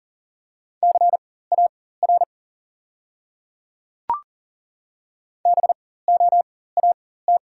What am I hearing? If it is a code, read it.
Morse code, heard as a keyed tone, about 40 wpm, spaced as for 19 wpm, 690 Hz. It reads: CAR BOAT
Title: 4x Character Spacing - Top 500 Words - 40wpm